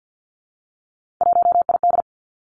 Morse code, heard as a keyed tone, about 50 wpm, 700 Hz. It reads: 1ID
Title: Three Challenging Letter-Number Combinations 50wpm